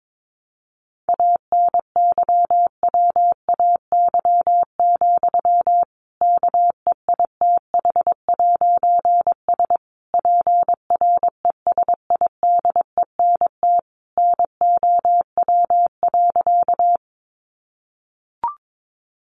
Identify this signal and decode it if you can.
Morse code, heard as a keyed tone, about 22 wpm, 700 Hz. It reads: ANYWAY, KEITH'S PRESIDENT NOW.
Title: A Sound Of Thunder 22wpm